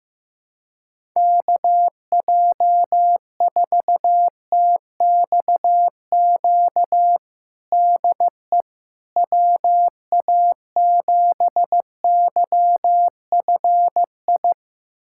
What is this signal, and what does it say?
Morse code, heard as a keyed tone, about 16 wpm, 700 Hz. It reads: KJ4TXQ DE WA7YFI
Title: USA Callsign DE Callsign - 6 Characters - 15wpm